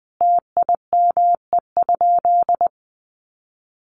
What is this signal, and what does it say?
Morse code, heard as a keyed tone, about 20 wpm, 700 Hz. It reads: TIME?
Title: Sentences from Top 400 Words 20wpm